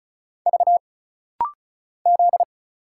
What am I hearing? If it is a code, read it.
Morse code, heard as a keyed tone, about 34 wpm, 700 Hz. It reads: V Z
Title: Single Letter-Number Alt - Rapid-Fire 35wpm